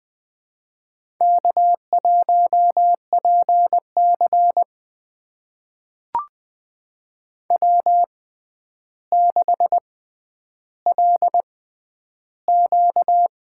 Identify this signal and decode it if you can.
Morse code, heard as a keyed tone, about 20 wpm, spaced as for 10 wpm, 700 Hz. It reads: K1PC W6LQ
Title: USA Call Signs - 6x Character Spacing - 20wpm